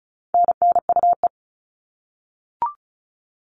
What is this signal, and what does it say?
Morse code, heard as a keyed tone, about 34 wpm, 700 Hz. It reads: NNUE
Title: All Cut Numbers - 4-Digit 35wpm